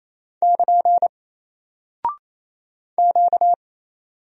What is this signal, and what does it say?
Morse code, heard as a keyed tone, about 28 wpm, 700 Hz. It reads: <KN> Q
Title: Single Character (Letter:Number:Punctuation:Prosign) 28wpm